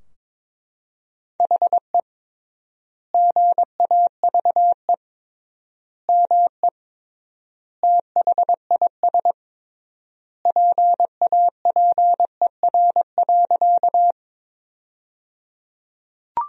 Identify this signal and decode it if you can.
Morse code, heard as a keyed tone, about 22 wpm, 700 Hz. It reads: HE GAVE ME THIS PAPER.
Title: 3x Word-Spacing - Sentences from Top 500 Words - 22wpm